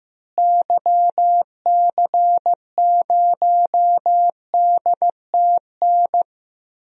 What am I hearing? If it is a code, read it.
Morse code, heard as a keyed tone, about 15 wpm, 700 Hz. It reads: YC0DTN
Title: DXCC-100 Call Signs - 6 Characters - Repeated 2x - 15wpm